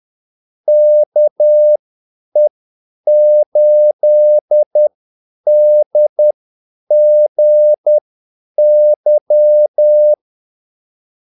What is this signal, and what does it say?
Morse code, heard as a keyed tone, about 10 wpm, spaced as for 8 wpm, 600 Hz. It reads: KE8DGY
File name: Callsigns-10-8wpm